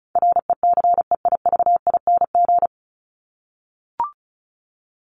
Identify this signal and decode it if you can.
Morse code, heard as a keyed tone, about 34 wpm, 700 Hz. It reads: RECEIVING
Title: ING Suffix - Top 100 Words with Base Words 35wpm